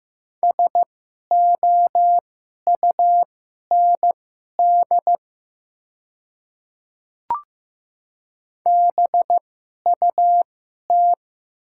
Morse code, heard as a keyed tone, about 15 wpm, spaced as for 11 wpm, 700 Hz. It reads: SOUND BUT
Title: 2x Character Spacing - Top 100 Words - 15wpm